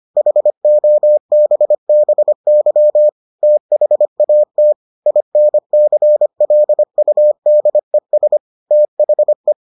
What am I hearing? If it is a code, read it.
Morse code, heard as a keyed tone, about 25 wpm, 600 Hz. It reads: HOBBY THAT INCLUDES THE